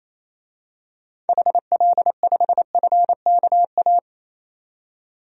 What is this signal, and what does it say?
Morse code, heard as a keyed tone, about 28 wpm, 700 Hz. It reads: HL5FKA
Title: DXCC-100 Call Signs - 6 Characters - 28wpm